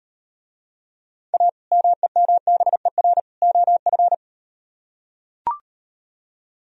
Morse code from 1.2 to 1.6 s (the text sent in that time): A